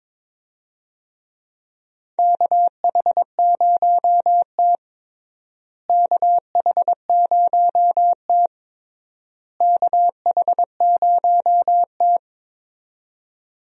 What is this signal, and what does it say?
Morse code, heard as a keyed tone, about 22 wpm, 700 Hz. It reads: KH0T KH0T KH0T
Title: USA Call Signs - 4 Characters - Repeated 3x - 22wpm